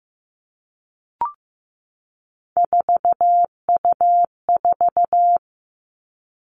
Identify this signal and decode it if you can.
Morse code, heard as a keyed tone, about 15 wpm, 700 Hz. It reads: 4U4
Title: Three Challenging Letter-Number Combinations 15wpm